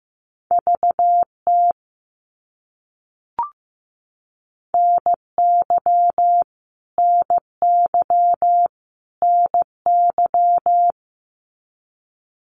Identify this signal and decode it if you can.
Morse code, heard as a keyed tone, about 15 wpm, 700 Hz. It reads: VT NY NY NY
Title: US States Repeated 3x 15wpm